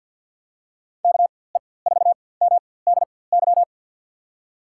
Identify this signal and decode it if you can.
Morse code, heard as a keyed tone, about 52 wpm, spaced as for 24 wpm, 700 Hz. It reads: KE4MDY
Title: USA Call Signs - 6 Characters - 4x Character Spacing - 50wpm